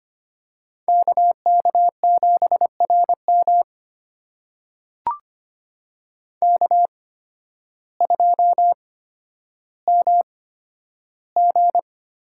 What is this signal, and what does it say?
Morse code, heard as a keyed tone, about 25 wpm, spaced as for 12 wpm, 700 Hz. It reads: KK7RM K2MG
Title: USA Call Signs - 5 Characters - 8x Character Spacing - 25wpm